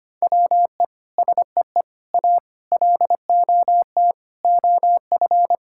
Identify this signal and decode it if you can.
Morse code, heard as a keyed tone, about 25 wpm, 700 Hz. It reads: WE SEE A LOT OF
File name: practice / 25wpm / Sentences with 9-Word Phrases - 25wpm